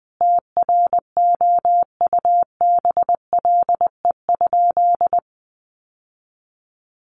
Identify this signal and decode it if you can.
Morse code, heard as a keyed tone, about 20 wpm, 700 Hz. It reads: TROUBLE?
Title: Sentences from Top 1200 Words 20wpm